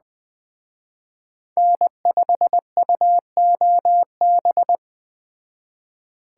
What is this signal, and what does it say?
Morse code, heard as a keyed tone, about 21 wpm, 700 Hz. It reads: N5UOB
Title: Speed Racing - USA 1x3 Call Signs 17wpm